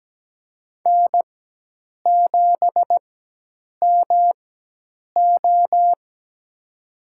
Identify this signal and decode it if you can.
Morse code, heard as a keyed tone, about 17 wpm, spaced as for 8 wpm, 700 Hz. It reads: N7MO
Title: USA Call Signs - 4 Characters - 4x Character Spacing - 17wpm